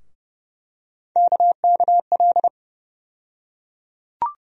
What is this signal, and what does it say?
Morse code, heard as a keyed tone, about 30 wpm, 700 Hz. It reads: KKL
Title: USA Call Sign Postfixes - 3 Letters 30wpm